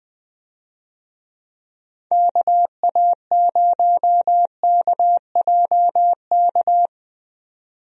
Morse code, heard as a keyed tone, about 20 wpm, 700 Hz. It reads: KA0KJK
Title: USA Call Signs - 6 Characters - 20wpm